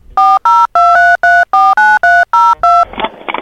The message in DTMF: 4023349302